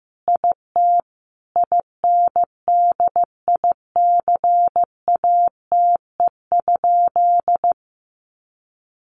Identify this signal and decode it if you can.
Morse code, heard as a keyed tone, about 15 wpm, 700 Hz. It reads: IT INDICATE?
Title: Sentences from Top 1300 Words 15wpm